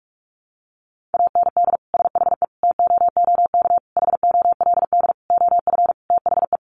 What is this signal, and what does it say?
Morse code, heard as a keyed tone, about 44 wpm, 700 Hz. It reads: AND SHE TOOK HOLD OF THE
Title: Sentences from Top 1200 Words 45wpm